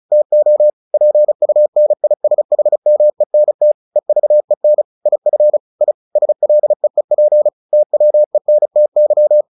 Morse code, heard as a keyed tone, about 34 wpm, 600 Hz. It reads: TO PUNISHMENT EVEN IF I SLEEP TWENTY
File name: the old man and the sea 2 35wpm